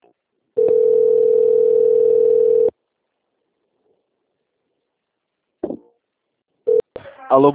Ringback tone (North America), 440 + 480 Hz.